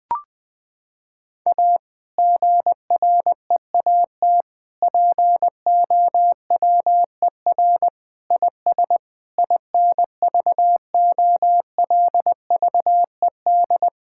Morse code, heard as a keyed tone, about 21 wpm, 700 Hz. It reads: A GREAT POWER IS INVOLVED
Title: Sentences from Top 1000 Words - Review Part 5 - 20wpm